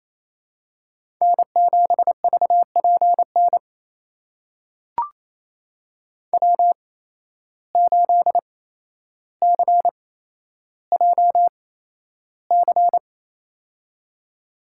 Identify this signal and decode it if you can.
Morse code, heard as a keyed tone, about 28 wpm, spaced as for 12 wpm, 700 Hz. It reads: N7VPN W8CJC
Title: USA Call Signs - 5 Characters - 8x Character Spacing - 28wpm